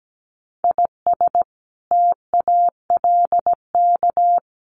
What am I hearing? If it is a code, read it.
Morse code, heard as a keyed tone, about 17 wpm, 700 Hz. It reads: IS TALK
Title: Sets of 5 Words Encore2 17wpm